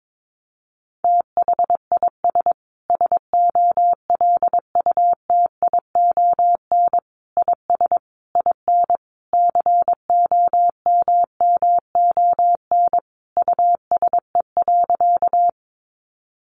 Morse code, heard as a keyed tone, about 22 wpm, 700 Hz. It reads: THIS SOLUTION IS IN COMMON USE.
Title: Sentences from Top 1300 Words 22wpm